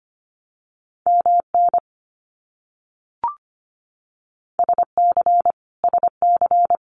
Morse code, heard as a keyed tone, about 24 wpm, 700 Hz. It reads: MN SC SC
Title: US States Repeated 3x 25wpm